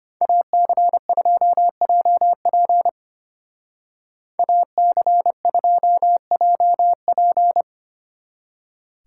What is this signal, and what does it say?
Morse code, heard as a keyed tone, about 32 wpm, 700 Hz. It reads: AC2JP AC2JP